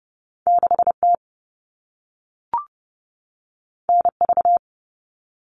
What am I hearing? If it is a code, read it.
Morse code, heard as a keyed tone, about 30 wpm, 700 Hz. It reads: 6T NV
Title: All Cut Numbers - 2-Digit 30wpm